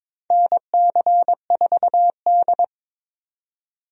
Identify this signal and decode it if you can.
Morse code, heard as a keyed tone, about 22 wpm, 700 Hz. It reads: NC4D
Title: USA Call Signs - 4 Characters - 22wpm